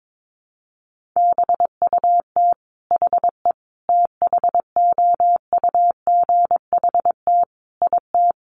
Morse code, heard as a keyed tone, about 22 wpm, 700 Hz. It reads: BUT HE THOUGHT IT
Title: Sentences from Top 600 Words 22wpm